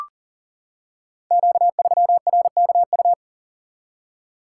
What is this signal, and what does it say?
Morse code, heard as a keyed tone, about 39 wpm, 690 Hz. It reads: Q3RKU
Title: Code Groups - 5 Letter:Numbers - 40wpm